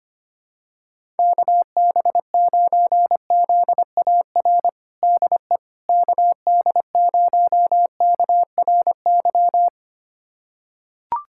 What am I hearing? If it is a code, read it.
Morse code, heard as a keyed tone, about 25 wpm, 700 Hz. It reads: KB9ZAR DE KD0KRY
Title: USA Callsign DE Callsign - 6 Characters - 25wpm